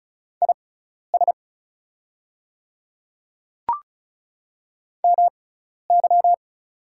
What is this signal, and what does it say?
Morse code, heard as a keyed tone, about 35 wpm, spaced as for 12 wpm, 690 Hz. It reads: IS MY
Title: 6x Character Spacing - Top 200 Words - 35wpm